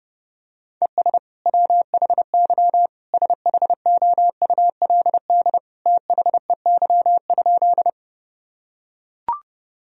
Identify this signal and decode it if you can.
Morse code, heard as a keyed tone, about 32 wpm, 700 Hz. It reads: ES WHY SHOULD THEY?
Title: Sentences from Top 300 Words and Abbreviations 30wpm